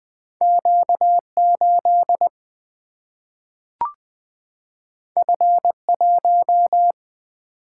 Morse code, heard as a keyed tone, about 20 wpm, 700 Hz. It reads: Q8 F1
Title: Single-Letter Single-Number 20wpm